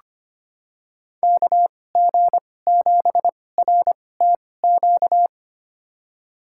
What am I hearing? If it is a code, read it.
Morse code, heard as a keyed tone, about 25 wpm, spaced as for 18 wpm, 700 Hz. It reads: KG7RTQ
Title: USA Call Signs - 6 Characters - 2x Character Spacing - 25wpm